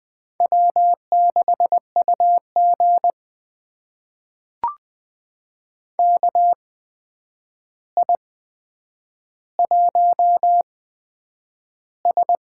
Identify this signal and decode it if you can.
Morse code, heard as a keyed tone, about 20 wpm, spaced as for 9 wpm, 700 Hz. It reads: W6UG KI1S
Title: USA Call Signs - 4 Characters - 8x Character Spacing - 20wpm